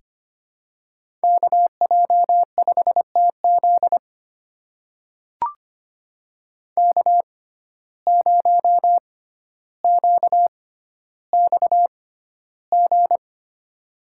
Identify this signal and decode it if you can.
Morse code, heard as a keyed tone, about 25 wpm, spaced as for 13 wpm, 700 Hz. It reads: KJ5TZ K0QXG
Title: USA Call Signs - 5 Characters - 6x Character Spacing - 25wpm